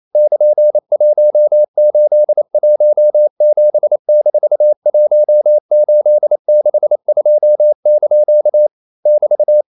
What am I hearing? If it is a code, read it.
Morse code, heard as a keyed tone, about 28 wpm, 600 Hz. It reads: <KN>1817-1862) <BT>
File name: CW / Practice / EnglishQuotations0228